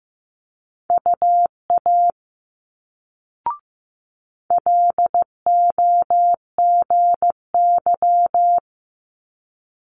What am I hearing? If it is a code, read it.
Morse code, heard as a keyed tone, about 15 wpm, 700 Hz. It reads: UA LOGY
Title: Multi-letter Phonemes Quick Fire 15wpm